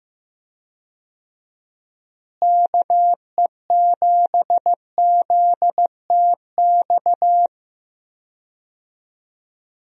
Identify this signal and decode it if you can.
Morse code, heard as a keyed tone, about 15 wpm, 700 Hz. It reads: KE7ZTX
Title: USA Call Signs - 6 Characters - Repeated 3x - 15wpm